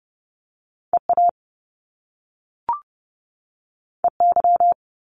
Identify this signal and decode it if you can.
Morse code, heard as a keyed tone, about 30 wpm, 700 Hz. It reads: EA EY
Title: Top Two Letter Combinations in Words 30wpm